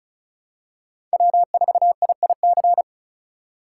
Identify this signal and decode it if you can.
Morse code, heard as a keyed tone, about 36 wpm, 700 Hz. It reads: W4IIC
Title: K1USN SST QSO Practice - 35wpm